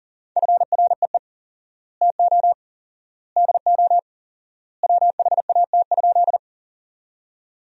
Morse code, heard as a keyed tone, about 40 wpm, 700 Hz. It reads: FREE TO DO WHAT?
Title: 4x Word-Spacing - Sentences from Top 500 Words - 40wpm